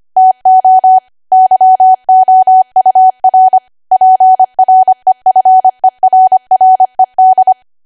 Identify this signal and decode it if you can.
Morse code, heard as a keyed tone, about 25 wpm, 750 Hz. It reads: TO YOUR PREFERRED
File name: Archive / 25 WPM / 25WPM